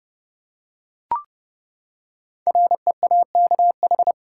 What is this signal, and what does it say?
Morse code, heard as a keyed tone, about 30 wpm, 700 Hz. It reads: REAKH
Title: Code Groups - 5 Letter:Numbers - 30wpm